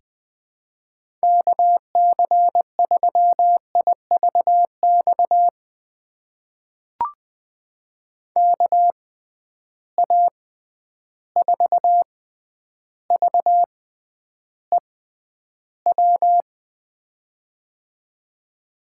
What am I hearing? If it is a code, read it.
Morse code, heard as a keyed tone, about 19 wpm, spaced as for 10 wpm, 700 Hz. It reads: KC3IVX KA4VEW
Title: USA Call Signs - 6 Characters - 6x Character Spacing - 20wpm